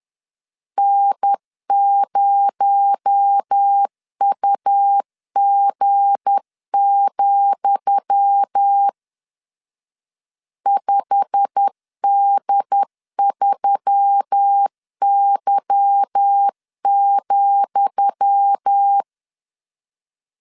Morse code, heard as a keyed tone, about 10 wpm, 800 Hz. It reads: N0UG, 5D3Y,
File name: LEZIONI / 10 SIGNS / 10D COMA 10wpm 3013